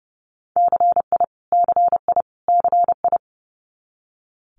Morse code, heard as a keyed tone, about 32 wpm, 700 Hz. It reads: CI CI CI
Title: Multi-letter Phonemes Repeated 3x 30wpm